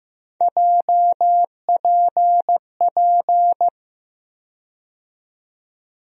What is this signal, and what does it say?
Morse code, heard as a keyed tone, about 15 wpm, 700 Hz. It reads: JPP
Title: USA Call Sign Postfixes - 3 Letters 15wpm